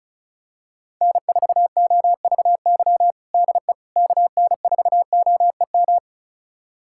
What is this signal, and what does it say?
Morse code, heard as a keyed tone, about 36 wpm, 700 Hz. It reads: N4OVY DE KN4OEM
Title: USA Callsign DE Callsign 35wpm